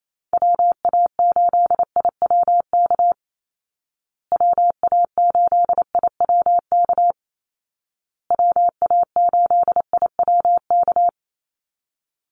Morse code, heard as a keyed tone, about 28 wpm, 700 Hz. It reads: WA8IWK WA8IWK WA8IWK
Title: Call Signs Repeated 2x - 28wpm